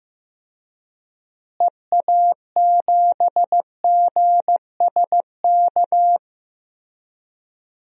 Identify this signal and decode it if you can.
Morse code, heard as a keyed tone, about 15 wpm, 700 Hz. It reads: EA7GSK
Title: DXCC-100 Call Signs - 6 Characters - Repeated 3x - 15wpm